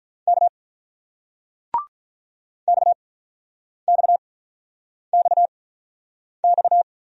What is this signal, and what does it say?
Morse code, heard as a keyed tone, about 46 wpm, 700 Hz. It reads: K X X X X